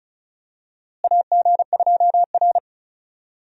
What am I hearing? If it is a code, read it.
Morse code, heard as a keyed tone, about 36 wpm, 700 Hz. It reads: AG2R
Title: USA Call Signs - 4 Characters - Repeated 3x - 35wpm